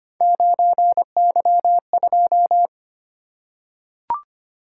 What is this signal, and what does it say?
Morse code, heard as a keyed tone, about 25 wpm, 700 Hz. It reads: 9Y2
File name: Three Letter-Number Combinations - 25wpm